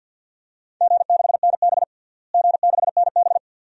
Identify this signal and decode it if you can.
Morse code, heard as a keyed tone, about 46 wpm, 700 Hz. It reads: G6NB G6NB
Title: DXCC-100 Call Signs - 4 Characters - Repeated 2x - 50wpm